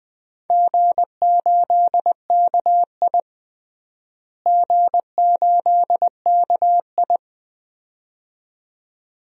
Morse code, heard as a keyed tone, about 20 wpm, 700 Hz. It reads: G8KI G8KI
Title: DXCC-100 Call Signs - 4 Characters - Repeated 2x - 20wpm